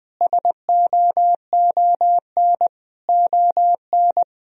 Morse code, heard as a keyed tone, about 20 wpm, 700 Hz. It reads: SOON ON
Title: Sentences from Top 1000 Words - Review Part 4 - 20wpm